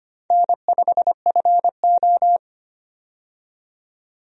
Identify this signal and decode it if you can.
Morse code, heard as a keyed tone, about 25 wpm, 700 Hz. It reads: N5FO
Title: USA Call Signs - 4 Characters - 25wpm